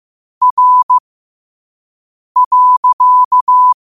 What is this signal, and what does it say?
Morse code, heard as a keyed tone, about 15 wpm, spaced as for 5 wpm, 1000 Hz. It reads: R.
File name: qotdfiles / 5wpm